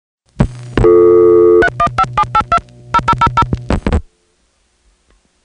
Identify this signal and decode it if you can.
DTMF, heard as telephone touch tones, tones of about 70 ms. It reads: 315782 0000